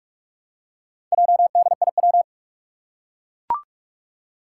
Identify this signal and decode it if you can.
Morse code, heard as a keyed tone, about 44 wpm, 700 Hz. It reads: JDIW
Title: Four Challenging Letter-Number Combinations 45wpm